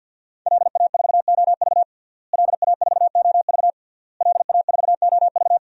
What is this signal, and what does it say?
Morse code, heard as a keyed tone, about 51 wpm, 700 Hz. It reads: LA4OV LA4OV LA4OV
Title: DXCC-100 Call Signs - 5 Characters - Repeated 3x - 50wpm